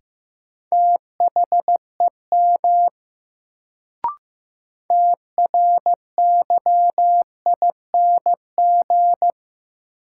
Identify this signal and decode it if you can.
Morse code, heard as a keyed tone, about 15 wpm, 700 Hz. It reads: THEM TRYING
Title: Sets of 2 Words - New Word Drill - 15wpm